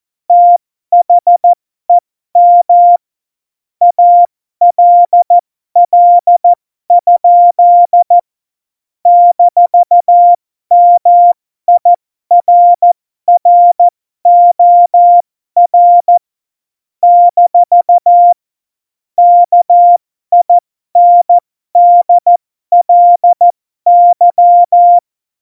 Morse code, heard as a keyed tone, about 14 wpm, spaced as for 12 wpm, 700 Hz. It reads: THEM ALL? -MIRROR - KINDLY
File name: Morsekurs / Witze